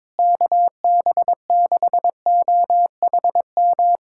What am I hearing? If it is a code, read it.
Morse code, heard as a keyed tone, about 22 wpm, 700 Hz. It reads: KB6OHM
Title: USA Callsign DE Callsign 22wpm